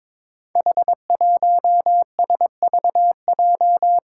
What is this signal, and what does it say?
Morse code, heard as a keyed tone, about 22 wpm, 700 Hz. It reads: H1SVJ
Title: Five Challenging Letter-Number Combinations 22wpm